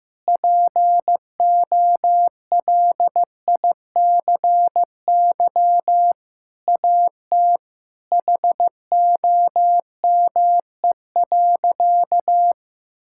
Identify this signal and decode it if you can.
Morse code, heard as a keyed tone, about 15 wpm, 700 Hz. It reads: POLICY AT HOME.
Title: Sentences from Top 1000 Words - Review Part 5 - 15wpm